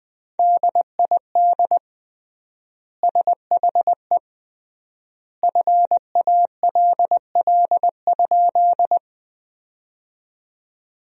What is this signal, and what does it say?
Morse code, heard as a keyed tone, about 20 wpm, 700 Hz. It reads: DID SHE FALL?
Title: 3x Word-Spacing - Sentences from Top 500 Words - 20wpm